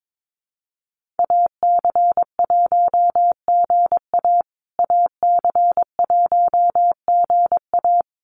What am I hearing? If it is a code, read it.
Morse code, heard as a keyed tone, about 22 wpm, 700 Hz. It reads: AC1GA AC1GA
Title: USA Call Signs Repeated 2x 22wpm